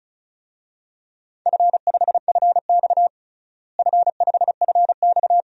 Morse code, heard as a keyed tone, about 36 wpm, 700 Hz. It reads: F5FX F5FX
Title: DXCC-100 Call Signs - 4 Characters - Repeated 3x - 35wpm